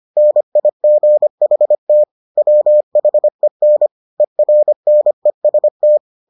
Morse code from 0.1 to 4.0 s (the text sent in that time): NIGHT WHEN